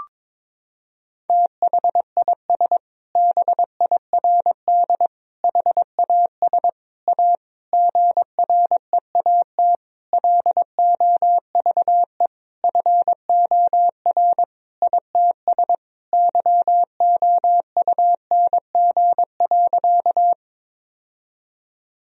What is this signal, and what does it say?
Morse code, heard as a keyed tone, about 22 wpm, 700 Hz. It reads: THIS BIRD HAS A GREAT LOVE FOR ITS YOUNG.
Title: Sentences from Top 1000 Words - Review Part 2 - 22wpm